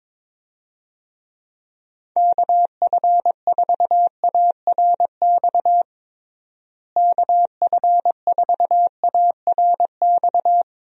Morse code, heard as a keyed tone, about 22 wpm, 700 Hz. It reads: KF4ARX KF4ARX
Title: USA Call Signs - 6 Characters - Repeated 2x - 22wpm